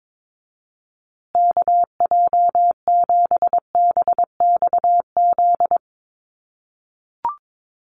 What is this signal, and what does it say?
Morse code, heard as a keyed tone, about 22 wpm, 700 Hz. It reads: KJ7BXZ
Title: USA Call Signs Repeated 3x 22wpm